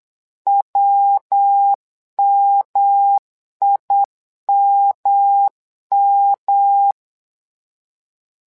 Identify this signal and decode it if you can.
Morse code, heard as a keyed tone, about 8 wpm, 800 Hz. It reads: WMIMM
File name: LEZIONI / 3 MWRO / 3 MWRO 8wpm 2606